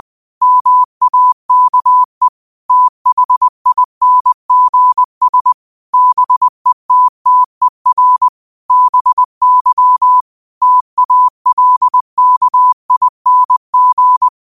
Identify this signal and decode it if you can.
Morse code, heard as a keyed tone, about 20 wpm, 1000 Hz. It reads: MAKE THINGS BETTER BY TALKING
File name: qotdfiles / 20wpm